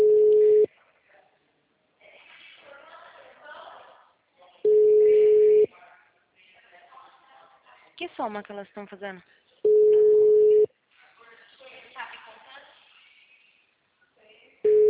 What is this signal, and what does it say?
Ringback tone (Europe, 425 Hz), with speech.
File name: out- -2034- - - .441